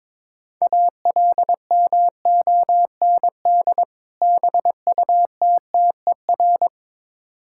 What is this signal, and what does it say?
Morse code, heard as a keyed tone, about 22 wpm, 700 Hz. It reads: ALMOND BUTTER
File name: Difficult Words with Hints - 22wpm